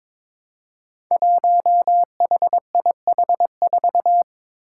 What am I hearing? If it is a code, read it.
Morse code, heard as a keyed tone, about 22 wpm, 700 Hz. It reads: 1HIH4
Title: Five Challenging Letter-Number Combinations 22wpm